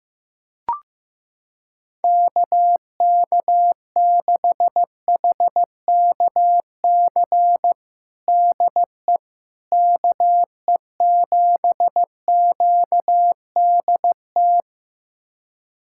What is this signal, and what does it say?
Morse code, heard as a keyed tone, about 15 wpm, 700 Hz. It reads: KK6HKC DE KE7QDT
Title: USA Callsign DE Callsign - 6 Characters - 15wpm